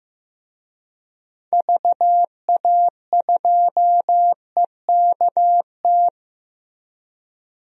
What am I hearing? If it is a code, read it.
Morse code, heard as a keyed tone, about 15 wpm, 700 Hz. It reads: VA2EKT
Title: DXCC-100 Call Signs - 6 Characters - Repeated 2x - 15wpm